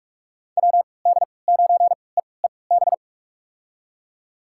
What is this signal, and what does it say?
Morse code, heard as a keyed tone, about 46 wpm, spaced as for 25 wpm, 700 Hz. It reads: WD9EEB
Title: USA Call Signs - 6 Characters - 3x Character Spacing - 45wpm